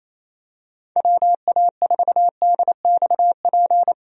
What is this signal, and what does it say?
Morse code, heard as a keyed tone, about 27 wpm, 700 Hz. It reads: WA4DXP
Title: USA Call Signs - 6 Characters - Repeated 3x - 28wpm